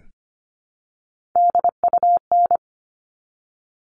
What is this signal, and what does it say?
Morse code, heard as a keyed tone, about 25 wpm, 700 Hz. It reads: DUN
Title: All Cut Numbers - 3-Digit 25wpm